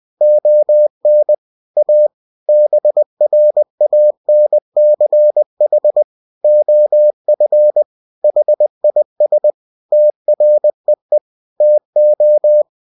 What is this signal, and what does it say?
Morse code, heard as a keyed tone, about 20 wpm, 600 Hz. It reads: ON A BRANCH OF HIS TREE TO
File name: cw-books / winnie-the-pooh / winnie-20wpm-page-0101